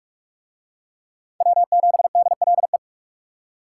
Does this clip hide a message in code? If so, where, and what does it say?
Yes: Morse code from 1.4 to 2.8 s, heard as a keyed tone, about 41 wpm, 700 Hz. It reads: W7DLE